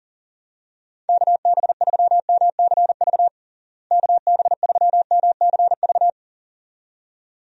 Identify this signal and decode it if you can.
Morse code, heard as a keyed tone, about 37 wpm, 700 Hz. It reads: KB3MCV KB3MCV
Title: USA Call Signs - 6 Characters - Repeated 2x - 40wpm